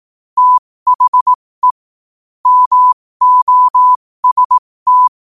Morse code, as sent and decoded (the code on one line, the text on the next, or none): - .... / -- --- ... -
THE MOST